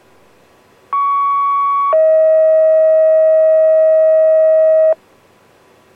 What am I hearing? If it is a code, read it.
Two-tone page, heard as a one-second tone, then a three-second tone, about 1120 then 640 Hz. It is heard from 0.9 to 4.9 s.